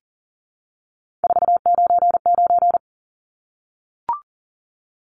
Morse code, heard as a keyed tone, about 40 wpm, 700 Hz. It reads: 499
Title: Numbers - 3-Digit 40wpm